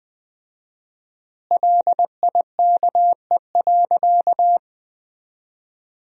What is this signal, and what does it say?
Morse code, heard as a keyed tone, about 21 wpm, 700 Hz. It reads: LIKE.
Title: 4x Word-Spacing - Sentences from Top 100 Words - 20wpm